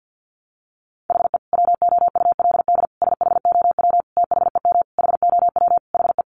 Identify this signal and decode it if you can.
Morse code, heard as a keyed tone, about 50 wpm, 700 Hz. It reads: HE WOULD SHOW THEM HOW HE
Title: Sentences from Top 1300 Words 50wpm